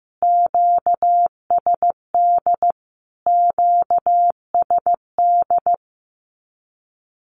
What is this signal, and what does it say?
Morse code, heard as a keyed tone, about 15 wpm, 700 Hz. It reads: QSD QSD
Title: Q-Codes Repeated 3x 15wpm